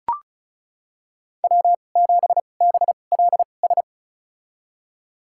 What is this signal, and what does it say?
Morse code, heard as a keyed tone, about 34 wpm, spaced as for 25 wpm, 700 Hz. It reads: W7BLS
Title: USA Call Signs - 5 Characters - 2x Character Spacing - 35wpm